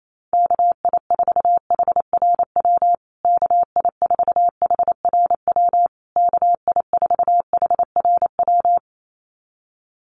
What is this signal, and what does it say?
Morse code, heard as a keyed tone, about 28 wpm, 700 Hz. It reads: KI4HRW KI4HRW KI4HRW
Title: USA Call Signs Repeated 3x 28wpm